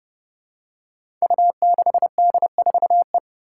Morse code, heard as a keyed tone, about 30 wpm, 700 Hz. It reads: U6D4E
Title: Five Challenging Letter-Number Combinations 30wpm